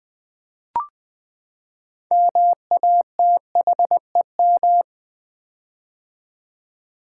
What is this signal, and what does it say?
Morse code, heard as a keyed tone, about 20 wpm, 700 Hz. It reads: MATHEM
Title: Build A Word - 6 Letters At A Time - 12-Letter Words - 20wpm